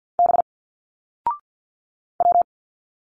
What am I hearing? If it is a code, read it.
Morse code, heard as a keyed tone, about 47 wpm, 700 Hz. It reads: B W
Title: Single Letters 50wpm